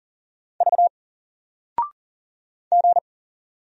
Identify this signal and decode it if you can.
Morse code, heard as a keyed tone, about 39 wpm, 700 Hz. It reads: V G
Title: Single Character (Letter:Number:Punctuation:Prosign) 40wpm